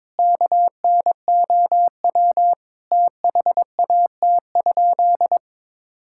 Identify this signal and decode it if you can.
Morse code, heard as a keyed tone, about 22 wpm, 700 Hz. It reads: KNOW THAT?
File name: practice / 22wpm / Sentences with 4-Word Phrases - 22wpm